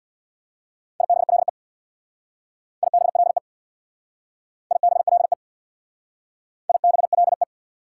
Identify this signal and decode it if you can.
Morse code, heard as a keyed tone, about 64 wpm, 700 Hz. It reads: IBLE IBLE IBLE IBLE